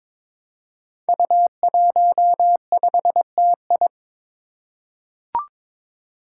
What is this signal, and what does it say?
Morse code, heard as a keyed tone, about 21 wpm, 700 Hz. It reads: U15TI
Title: Five Challenging Letter-Number Combinations 22wpm